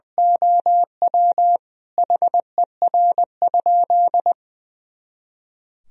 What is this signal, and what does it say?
Morse code, heard as a keyed tone, about 21 wpm, 700 Hz. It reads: OW HER?
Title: Sentences from Top 100 Words and Abbreviations 20wpm